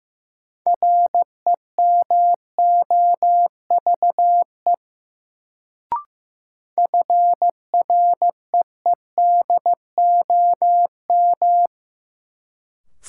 Morse code, heard as a keyed tone, about 16 wpm, 700 Hz. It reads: REMOVE FREEDOM
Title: Sentences from Top 800 Words - New Word Drill - 15wpm